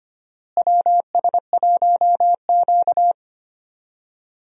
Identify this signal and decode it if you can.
Morse code, heard as a keyed tone, about 25 wpm, 700 Hz. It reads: WS1Q
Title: USA Call Signs - 4 Characters - Repeated 3x - 25wpm